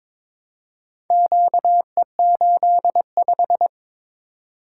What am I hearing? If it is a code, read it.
Morse code, heard as a keyed tone, about 22 wpm, 700 Hz. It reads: QE85
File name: practice / 22wpm / Four Letter-Number Combinations - 22wpm